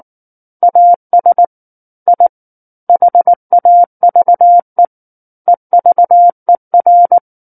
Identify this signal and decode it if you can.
Morse code, heard as a keyed tone, about 19 wpm, 700 Hz. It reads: AS I HAVE EVER
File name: Book One / 10-Chapter 10 - 19 WPM